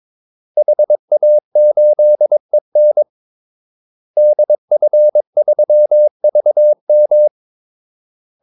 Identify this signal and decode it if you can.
Morse code, heard as a keyed tone, about 22 wpm, 600 Hz. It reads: HA8EN DF3VM
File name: Callsigns-22wpm